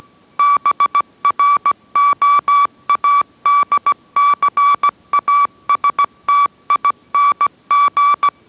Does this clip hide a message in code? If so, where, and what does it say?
Yes: Morse code from 0.4 to 8.3 s, heard as a keyed tone, about 17 wpm, 1150 Hz. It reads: BROADCASTING